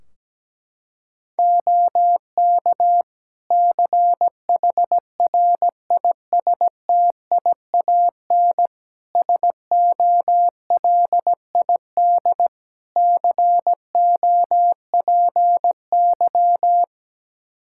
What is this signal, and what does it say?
Morse code, heard as a keyed tone, about 17 wpm, 700 Hz. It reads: OK CHRISTIAN SOLID COPY
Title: Practice QSO - Exchange 3 - More Info - 17wpm